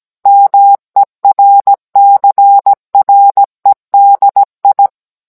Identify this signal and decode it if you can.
Morse code, heard as a keyed tone, about 17 wpm, 800 Hz. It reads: MERCREDI
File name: Cours-NR18-16mm